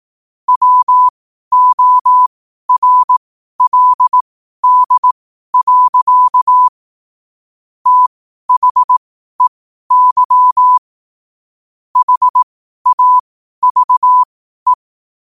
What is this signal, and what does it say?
Morse code, heard as a keyed tone, about 18 wpm, spaced as for 12 wpm, 1000 Hz. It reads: WORLD. THEY HAVE
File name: qotdfiles / 12wpm